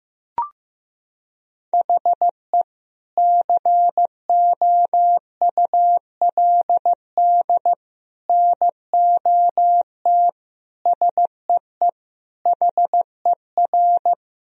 Morse code, heard as a keyed tone, about 16 wpm, 700 Hz. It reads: HE COULD NOT SEE HER
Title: Sentences from Top 1000 Words - Review Part 2 - 15wpm